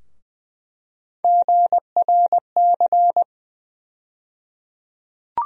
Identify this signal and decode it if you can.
Morse code, heard as a keyed tone, about 20 wpm, 700 Hz. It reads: GRC
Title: USA Call Sign Postfixes - 3 Letters 20wpm